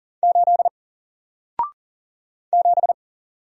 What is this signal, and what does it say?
Morse code, heard as a keyed tone, about 39 wpm, 700 Hz. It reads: 8 7
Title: Single Character (Letter:Number:Punctuation:Prosign) Alt 40wpm